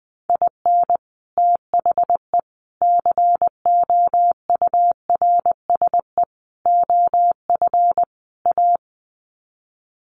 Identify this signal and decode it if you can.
Morse code, heard as a keyed tone, about 20 wpm, 700 Hz. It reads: IN THE COURSE OF A
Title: Sets of 5 Words Encore2 20wpm